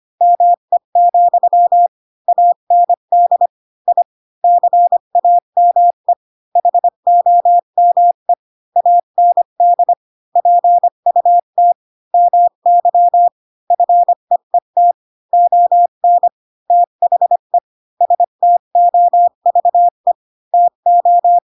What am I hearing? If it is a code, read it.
Morse code, heard as a keyed tone, about 25 wpm, 700 Hz. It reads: ME, AND I CAME HOME AND PUT MY FEET ON THE STOVE TO
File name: Pinocchio / Pinocchio ch7 20 25 ik1hge